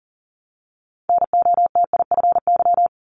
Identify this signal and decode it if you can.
Morse code, heard as a keyed tone, about 40 wpm, 700 Hz. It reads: NOTIFY
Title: Multi-letter Phonemes 40wpm